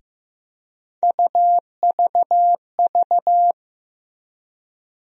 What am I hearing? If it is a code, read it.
Morse code, heard as a keyed tone, about 15 wpm, 700 Hz. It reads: UVV
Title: USA Call Sign Postfixes - 3 Letters 15wpm